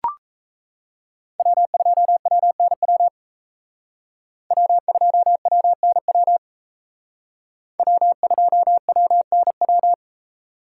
Morse code, heard as a keyed tone, about 34 wpm, 700 Hz. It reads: W2WNW W2WNW W2WNW
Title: Speed Racing - USA 1x3 Call Signs 28wpm